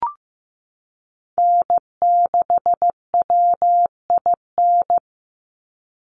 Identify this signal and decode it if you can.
Morse code, heard as a keyed tone, about 15 wpm, 700 Hz. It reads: N6WIN
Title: Call Signs Repeated 2x - 15wpm